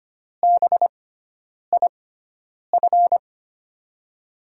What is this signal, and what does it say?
Morse code, heard as a keyed tone, about 25 wpm, spaced as for 9 wpm, 700 Hz. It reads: BIF
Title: USA Call Signs - 6x Character Spacing - 25wpm